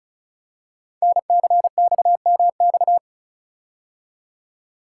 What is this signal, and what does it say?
Morse code, heard as a keyed tone, about 35 wpm, 700 Hz. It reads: NCXMX
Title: Code Groups - 5 Letters - 35wpm